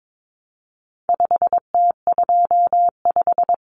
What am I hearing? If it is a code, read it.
Morse code, heard as a keyed tone, about 22 wpm, 700 Hz. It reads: 5T25
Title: Common Cut Numbers - 4-Digit 22wpm